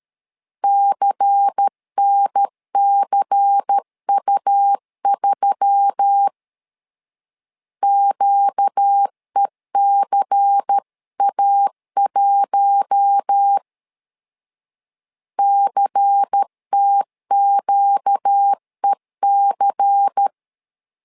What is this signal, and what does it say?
Morse code, heard as a keyed tone, about 13 wpm, 800 Hz. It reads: CNCU3 QECA1 CTQEC
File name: LEZIONI / 5 LBVC / 5 LBVC 12wpm 1955